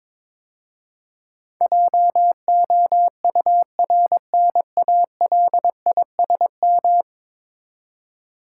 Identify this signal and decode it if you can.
Morse code, heard as a keyed tone, about 22 wpm, 700 Hz. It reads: JOURNALISM
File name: practice / 22wpm / 10 Letter Words - First 3 Letters Given - 22wpm